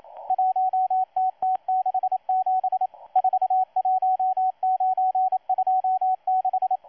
Morse code, heard as a keyed tone, about 27 wpm, 750 Hz, with radio noise. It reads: 1TT67 41926
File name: M12-2016-07-07-1310utc-13926khz-error-restart